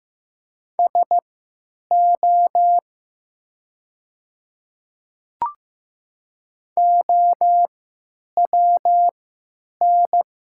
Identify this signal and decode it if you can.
Morse code, heard as a keyed tone, about 15 wpm, spaced as for 9 wpm, 700 Hz. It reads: SO OWN
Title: 3x Character Spacing - Top 300 Words - 15wpm